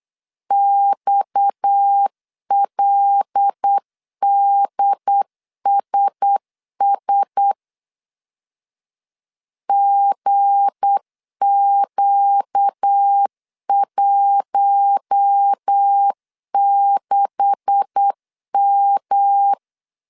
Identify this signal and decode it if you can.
Morse code, heard as a keyed tone, about 8 wpm, 800 Hz. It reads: XLDSS GQ16M
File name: CW / DET / 1 9 1500C S C 8wpm 5653